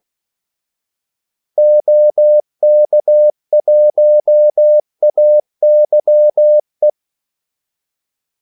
Morse code, heard as a keyed tone, about 16 wpm, 600 Hz. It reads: OK1AYE
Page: Kaldesignaler 16 wpm